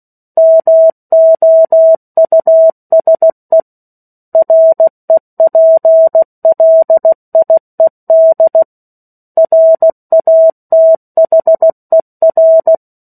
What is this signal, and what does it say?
Morse code, heard as a keyed tone, about 16 wpm, 650 Hz. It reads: MOUSE REPLIED RATHER
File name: 03-Chapter-03 16 WPM